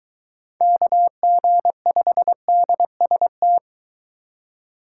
Morse code, heard as a keyed tone, about 22 wpm, 700 Hz. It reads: KG5DST